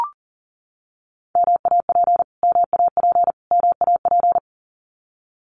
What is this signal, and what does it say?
Morse code, heard as a keyed tone, about 40 wpm, 700 Hz. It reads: MAP MAP MAP
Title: Top 500 Words Repeated 40wpm